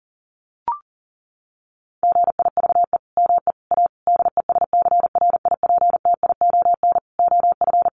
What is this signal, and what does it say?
Morse code, heard as a keyed tone, about 40 wpm, 700 Hz. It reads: GIVE ME A DESCRIPTION OF
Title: Sentences from Top 1300 Words 40wpm